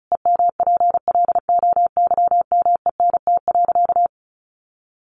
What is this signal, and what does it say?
Morse code, heard as a keyed tone, about 34 wpm, 700 Hz. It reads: EMPLOYMENT.
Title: Sentences from Top 1500 Words 35wpm